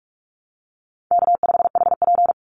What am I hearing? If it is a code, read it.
Morse code, heard as a keyed tone, about 44 wpm, 700 Hz. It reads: K5HP
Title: Easy Call Signs Repeated 3x - 45wpm